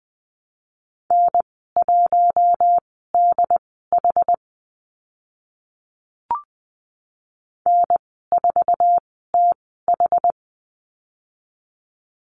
Morse code, heard as a keyed tone, about 20 wpm, spaced as for 15 wpm, 700 Hz. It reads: N1DH N4TH
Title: 088 Lesson 15g - Number 5 - Call Signs 20wpm